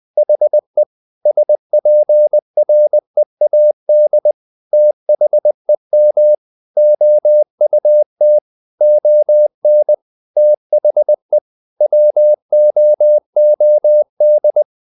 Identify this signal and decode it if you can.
Morse code, heard as a keyed tone, about 20 wpm, 600 Hz. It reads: HE SPREAD THEM OUT ON THE WOOD
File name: the old man and the sea / the old man and the sea part 1 20wpm